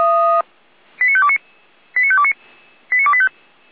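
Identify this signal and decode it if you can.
Five-tone selective call (ZVEI), heard as quick runs of five tones, tones of about 70 ms. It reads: 89529 89529 89285
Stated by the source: ZVEI five-tone selective call received on an SDR, fire brigade Feuerwehr Orlamünde.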